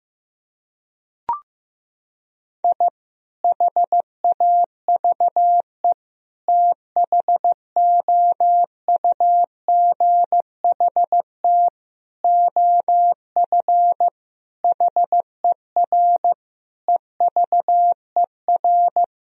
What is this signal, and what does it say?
Morse code, heard as a keyed tone, about 15 wpm, 700 Hz. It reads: I HAVE THOUGHT OF HER EVER
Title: Sentences from Top 1000 Words - Review Part 1 - 15wpm